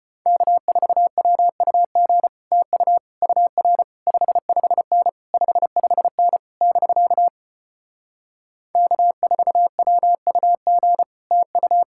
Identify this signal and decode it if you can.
Morse code, heard as a keyed tone, about 33 wpm, 700 Hz. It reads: K4WUG TU UR 55N 55N <BK> K4WUG TU